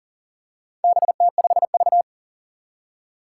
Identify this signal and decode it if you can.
Morse code, heard as a keyed tone, about 38 wpm, 700 Hz. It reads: BT5V
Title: Four Challenging Letter-Number Combinations 40wpm